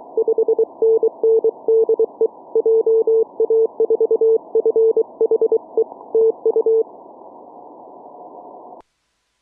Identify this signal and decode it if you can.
Morse code, heard as a keyed tone, about 23 wpm, 450 Hz. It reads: 5NN DE JA4FHE TU